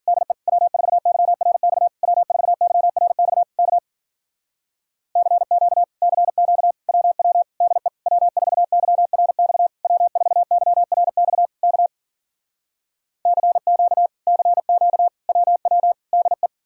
Morse code, heard as a keyed tone, about 50 wpm, 700 Hz. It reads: DE W4YRX W4YRX K CQ CQ WW DE W4YRX W4YRX K CQ CQ WW DE